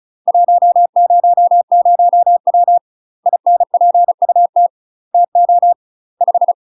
Morse code, heard as a keyed tone, about 34 wpm, 700 Hz. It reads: 100W INPUT TO 5
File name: UK MASTER QSOs TRAINING / 35WPM / UK MASTER QSOs 0040